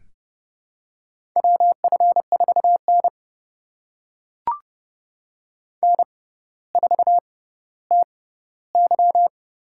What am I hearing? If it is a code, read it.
Morse code, heard as a keyed tone, about 30 wpm, spaced as for 15 wpm, 700 Hz. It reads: WF4N N4TY
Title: USA Call Signs - 4 Characters - 6x Character Spacing - 30wpm